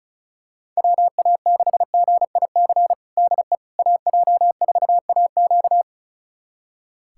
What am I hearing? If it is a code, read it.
Morse code, heard as a keyed tone, about 34 wpm, 700 Hz. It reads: WA6GIC DE AJ4AQ
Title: USA Callsign DE Callsign 35wpm